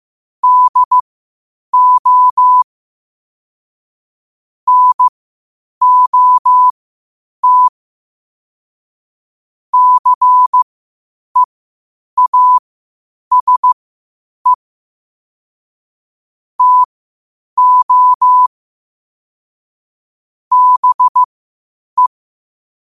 Morse code, heard as a keyed tone, about 15 wpm, spaced as for 8 wpm, 1000 Hz. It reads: DO NOT CEASE TO BE
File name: qotdfiles / 8wpm